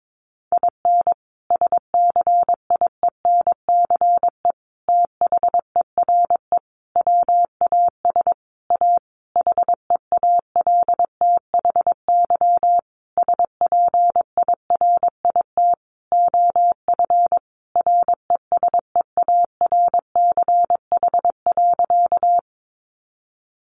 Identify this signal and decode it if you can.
Morse code, heard as a keyed tone, about 22 wpm, 700 Hz. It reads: IN SCIENCE THERE WAS A HEALTHY SPIRIT OF RESEARCH.
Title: Sentences from Top 1400 Words 22wpm